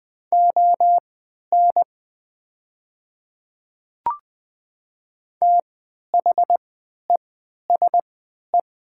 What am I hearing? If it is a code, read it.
Morse code, heard as a keyed tone, about 21 wpm, spaced as for 11 wpm, 700 Hz. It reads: ON THESE
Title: 3x Character Spacing - Top 100 Words - 20wpm